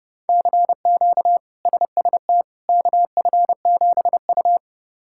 Morse code, heard as a keyed tone, about 30 wpm, 700 Hz. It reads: CQ SST KF7U
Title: K1USN SST QSO Practice - 30wpm